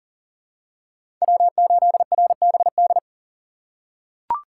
Morse code, heard as a keyed tone, about 40 wpm, 700 Hz. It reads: W8RBD
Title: USA Call Signs - 5 Characters - 2x Character Spacing - 40wpm